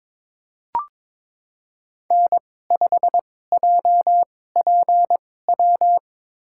Morse code, heard as a keyed tone, about 22 wpm, spaced as for 16 wpm, 700 Hz. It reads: N5JPW
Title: USA Call Signs - 5 Characters - 2x Character Spacing - 22wpm